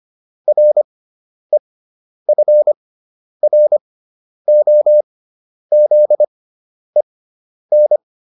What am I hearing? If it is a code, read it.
Morse code, heard as a keyed tone, about 25 wpm, spaced as for 10 wpm, 600 Hz. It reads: REFROZEN